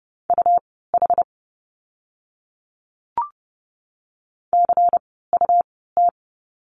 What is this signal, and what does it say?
Morse code, heard as a keyed tone, about 30 wpm, spaced as for 17 wpm, 700 Hz. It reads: UH CUT
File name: Lesson 17f - Letter u - Words 30wpm